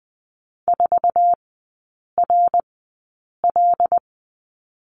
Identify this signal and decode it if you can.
Morse code, heard as a keyed tone, about 20 wpm, spaced as for 8 wpm, 700 Hz. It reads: 4RL
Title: 074 Lesson 13e - Letter L - Three Characters 20wpm